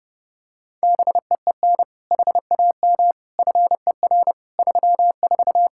Morse code, heard as a keyed tone, about 29 wpm, 700 Hz. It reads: BEEN HAM FER 34
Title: Practice QSO - Exchange 3 - More Info - 30wpm